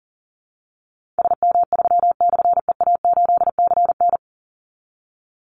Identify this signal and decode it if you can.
Morse code, heard as a keyed tone, about 40 wpm, 700 Hz. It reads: SM3/EA8CN
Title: Call Signs Repeated 3x - 40wpm